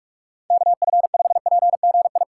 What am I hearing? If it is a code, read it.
Morse code, heard as a keyed tone, about 46 wpm, 700 Hz. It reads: KF5PGI
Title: USA Call Signs - 6 Characters - 6x Character Spacing - 45wpm